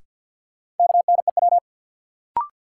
Morse code, heard as a keyed tone, about 53 wpm, 700 Hz. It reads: KNEW
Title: Sentences from Top 400 Words - New Word Drill - 50wpm